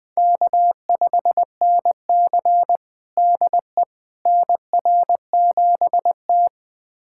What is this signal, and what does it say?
Morse code, heard as a keyed tone, about 20 wpm, 700 Hz. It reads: K5NC DE NR7T
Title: USA Callsign DE Callsign - 4 Characters - 20wpm